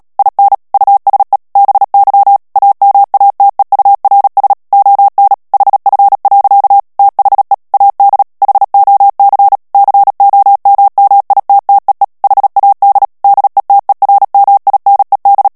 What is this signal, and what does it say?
Morse code, heard as a keyed tone, about 37 wpm, 800 Hz. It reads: IN USE BY AMATEURS ON HF. THE AD HOC COMMITTEE HAD DETERMINED
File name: morse mp3 / ANGLAIS texte clair 3 vitesse 35 mots minute